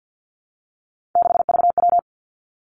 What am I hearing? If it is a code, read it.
Morse code, heard as a keyed tone, about 50 wpm, 700 Hz. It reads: 6VW
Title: Three Challenging Letter-Number Combinations 50wpm